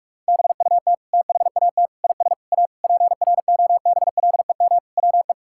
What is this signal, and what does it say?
Morse code, heard as a keyed tone, about 46 wpm, 700 Hz. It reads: BUT THAT IS A PROBLEM WE
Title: Sentences from Top 1000 Words - Review Part 6 - 45wpm